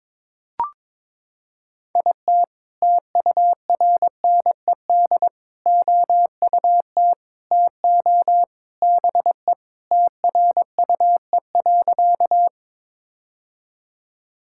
Morse code, heard as a keyed tone, about 21 wpm, 700 Hz. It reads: IT TURNED OUT TO BE TRUE.
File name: practice / 22wpm / Sentences with 3-Word Phrases - 22wpm